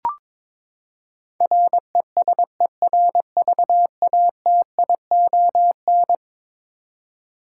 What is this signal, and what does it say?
Morse code, heard as a keyed tone, about 22 wpm, 700 Hz. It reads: RESERVATION
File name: practice / 22wpm / Eleven Letter Words - 22wpm